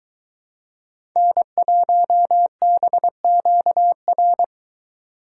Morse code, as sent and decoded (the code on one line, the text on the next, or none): -. .---- -... --.- .-.
N1BQR